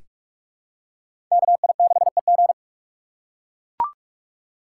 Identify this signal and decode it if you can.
Morse code, heard as a keyed tone, about 44 wpm, 690 Hz. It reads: KI6EG
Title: USA Call Signs - 5 Characters - 6x Character Spacing - 45wpm